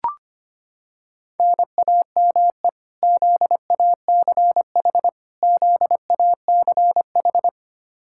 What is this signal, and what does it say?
Morse code, heard as a keyed tone, about 25 wpm, 700 Hz. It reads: NAME ZACH ZACH
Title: Practice QSO - Exchange 1 - Beginning - 25wpm